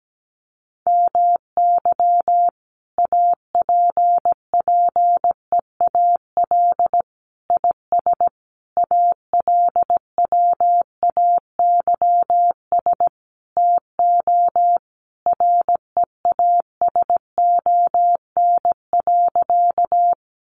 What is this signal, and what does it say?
Morse code, heard as a keyed tone, about 17 wpm, 700 Hz. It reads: MY APPEAL IS ALWAYS TO REASON.
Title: Sentences from Top 1300 Words 17wpm